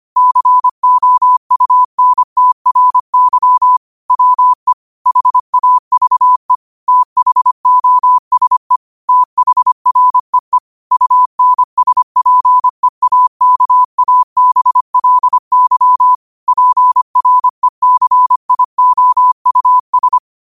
Morse code, heard as a keyed tone, about 25 wpm, 1000 Hz. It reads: COUNTRY WE HAVE THOSE THREE UNSPEAKABLY PRECIOUS
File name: qotdfiles / 25wpm